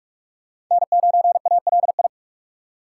Morse code, heard as a keyed tone, about 46 wpm, 700 Hz. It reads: N9ALI
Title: USA Call Signs - 3x Character Spacing - 45wpm